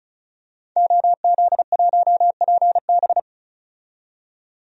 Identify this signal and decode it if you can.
Morse code, heard as a keyed tone, about 34 wpm, 700 Hz. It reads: OZ1PB
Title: DXCC-100 Call Signs - 5 Characters - 35wpm